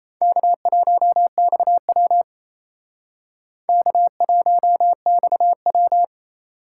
Morse code, heard as a keyed tone, about 36 wpm, 700 Hz. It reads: K1XW K1XW